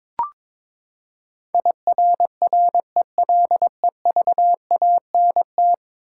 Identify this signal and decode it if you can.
Morse code, heard as a keyed tone, about 21 wpm, 700 Hz. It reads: IRRELEVANT
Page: morse-code-ninja-practice-22wpm